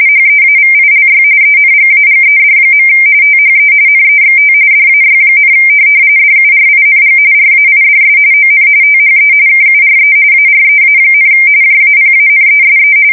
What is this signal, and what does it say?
RTTY, about 45.45 baud. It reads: HEAVILY ENGAGED. THE NEXT DAY, DURING RAIDS ON THE KURE AREA, JAPANESE
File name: fsk170 / USS-Massachusetts